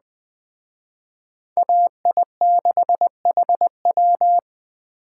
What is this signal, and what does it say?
Morse code, heard as a keyed tone, about 20 wpm, 700 Hz. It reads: AI6HW